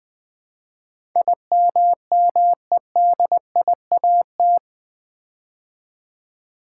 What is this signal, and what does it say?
Morse code, heard as a keyed tone, about 20 wpm, 700 Hz. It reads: IMMEDIAT
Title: Build A Word - 4 Letters At A Time - 11-Letter Words - 20wpm